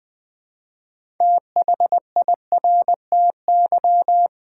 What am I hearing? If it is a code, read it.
Morse code, heard as a keyed tone, about 20 wpm, 700 Hz. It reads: THIRTY
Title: Build A Word - 2 Letters At A Time - 6-Letter Words - 20wpm